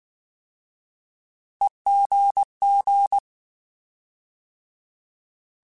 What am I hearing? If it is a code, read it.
Morse code, heard as a keyed tone, about 19 wpm, 800 Hz. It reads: EGG
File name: Alphabetical / Alphabetical mp3s / Es @18